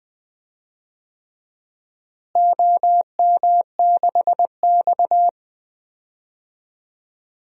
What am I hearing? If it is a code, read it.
Morse code, heard as a keyed tone, about 20 wpm, 700 Hz. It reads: OM6X
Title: DXCC-100 Call Signs - 4 Characters - Repeated 3x - 20wpm